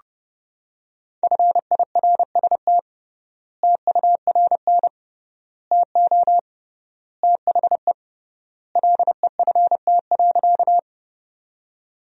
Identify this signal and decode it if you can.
Morse code, heard as a keyed tone, about 30 wpm, 700 Hz. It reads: FIRST TURN TO THE LEFT.
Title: 3x Word-Spacing - Sentences from Top 300 Words - 30wpm